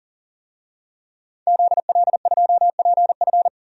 Morse code, heard as a keyed tone, about 40 wpm, 700 Hz. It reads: ZL2PF
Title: DXCC-100 Call Signs - 5 Characters - 40wpm